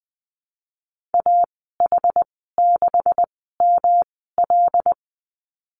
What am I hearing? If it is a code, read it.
Morse code, heard as a keyed tone, about 20 wpm, spaced as for 14 wpm, 700 Hz. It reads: AH6ML
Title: 128 Lesson 21g - Number 6 - Call Signs 20wpm